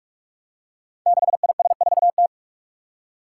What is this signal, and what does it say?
Morse code, heard as a keyed tone, about 46 wpm, 700 Hz. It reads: 6IS4T
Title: Five Challenging Letter-Number Combinations 45wpm